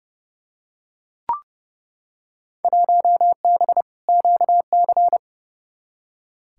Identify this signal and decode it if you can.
Morse code, heard as a keyed tone, about 30 wpm, 700 Hz. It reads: 1B QC
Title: Field Day Exchanges Encore - 30wpm